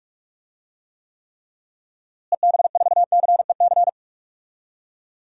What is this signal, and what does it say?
Morse code, heard as a keyed tone, about 46 wpm, 700 Hz. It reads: EB4CEC
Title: DXCC-100 Call Signs - 6 Characters - Repeated 3x - 45wpm